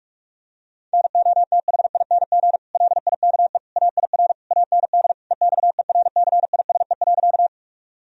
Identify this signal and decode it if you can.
Morse code, heard as a keyed tone, about 41 wpm, 700 Hz. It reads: NOTHING LIKE AIR AND EXERCISE.
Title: Sentences from Top 1000 Words - Review Part 4 - 45wpm